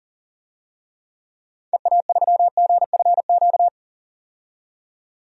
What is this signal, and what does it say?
Morse code, heard as a keyed tone, about 40 wpm, 700 Hz. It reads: EA3GFQ
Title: DXCC-100 Call Signs - 6 Characters - 40wpm